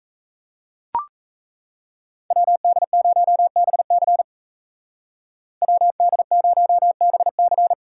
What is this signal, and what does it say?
Morse code, heard as a keyed tone, about 33 wpm, 700 Hz. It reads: WD0BC WD0BC
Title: Speed Racing - USA 2x2 Call Signs 28wpm